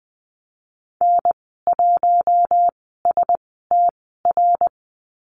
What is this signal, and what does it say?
Morse code, heard as a keyed tone, about 20 wpm, spaced as for 15 wpm, 700 Hz. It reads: N1STR
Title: 055 Lesson 10g - Letter r - Call Signs 20wpm